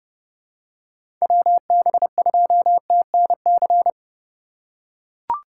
Morse code, heard as a keyed tone, about 30 wpm, 700 Hz. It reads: WB2TNC
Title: USA Call Signs - 6 Characters - 3x Character Spacing - 30wpm